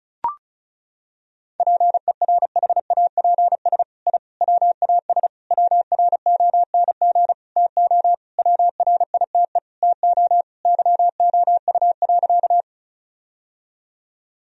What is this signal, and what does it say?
Morse code, heard as a keyed tone, about 34 wpm, 700 Hz. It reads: PERHAPS I WAS WRONG TO WRITE TO YOU.
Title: Sentences from Top 1000 Words - Review Part 1 - 35wpm